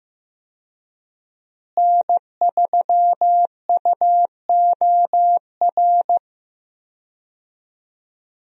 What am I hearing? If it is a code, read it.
Morse code, heard as a keyed tone, about 15 wpm, 700 Hz. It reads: N3UOR
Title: USA Call Signs - 5 Characters - Repeated 2x - 15wpm